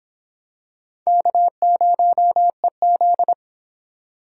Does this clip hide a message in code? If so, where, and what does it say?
Yes: Morse code from 1.1 to 3.3 s, heard as a keyed tone, about 24 wpm, 700 Hz. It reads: K0EZ